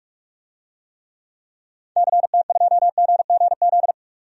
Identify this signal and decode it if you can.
Morse code, heard as a keyed tone, about 43 wpm, 700 Hz. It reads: CT2GGZ
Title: DXCC-100 Call Signs - 6 Characters - 45wpm